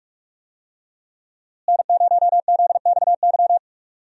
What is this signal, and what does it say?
Morse code, heard as a keyed tone, about 47 wpm, 700 Hz. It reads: N0ZXY
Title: USA Call Signs - 5 Characters - Repeated 3x - 45wpm